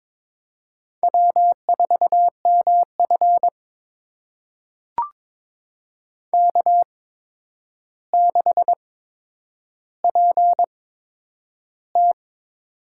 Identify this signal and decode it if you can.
Morse code, heard as a keyed tone, about 22 wpm, spaced as for 9 wpm, 700 Hz. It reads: W4MF K6PT
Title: USA Call Signs - 4 Characters - 8x Character Spacing - 22wpm